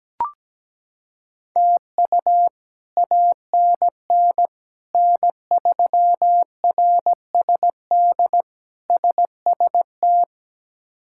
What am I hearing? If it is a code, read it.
Morse code, heard as a keyed tone, about 17 wpm, 700 Hz. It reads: TU ANN N3RSD SST
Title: K1USN SST QSO Practice - 17wpm